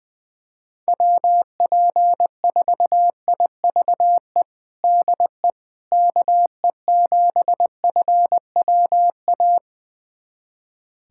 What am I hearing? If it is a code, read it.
Morse code, heard as a keyed tone, about 20 wpm, 700 Hz. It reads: WP4IVE DE KE7FWA
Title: USA Callsign DE Callsign 20wpm